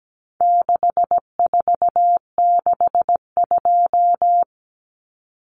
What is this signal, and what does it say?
Morse code, heard as a keyed tone, about 17 wpm, 700 Hz. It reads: 6462
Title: Numbers - 4-Digit 17wpm